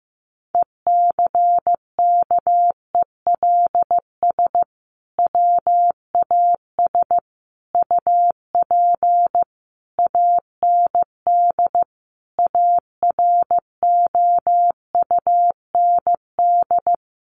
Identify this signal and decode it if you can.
Morse code, heard as a keyed tone, about 15 wpm, 700 Hz. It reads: ECKELS WAS UP AND AROUND
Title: A Sound Of Thunder 15wpm